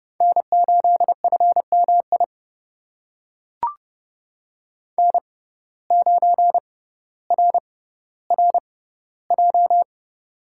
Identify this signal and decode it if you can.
Morse code, heard as a keyed tone, about 32 wpm, spaced as for 16 wpm, 700 Hz. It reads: N8FMI N9RRJ
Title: USA Call Signs - 5 Characters - 6x Character Spacing - 30wpm